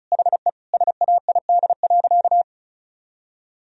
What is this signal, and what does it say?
Morse code, heard as a keyed tone, about 36 wpm, 700 Hz. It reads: HE SAID.
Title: Sentences from Top 1000 Words - Review Part 2 - 35wpm